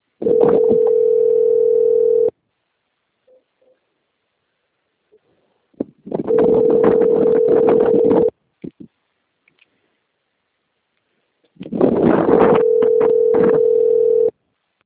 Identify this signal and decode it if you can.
Ringback tone (North America), 440 + 480 Hz.
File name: out- - - - - .38880